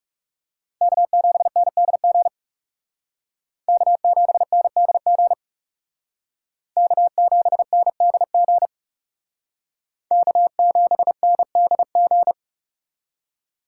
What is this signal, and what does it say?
Morse code, heard as a keyed tone, about 38 wpm, 700 Hz. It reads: K7NDG K7NDG K7NDG K7NDG